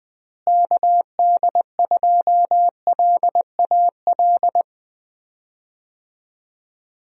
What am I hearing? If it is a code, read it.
Morse code, heard as a keyed tone, about 20 wpm, 700 Hz. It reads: KD2LAL
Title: USA Call Signs - 6 Characters - Repeated 2x - 20wpm